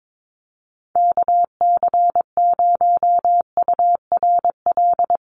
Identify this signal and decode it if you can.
Morse code, heard as a keyed tone, about 22 wpm, 700 Hz. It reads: KC0URL
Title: Call Signs - 22wpm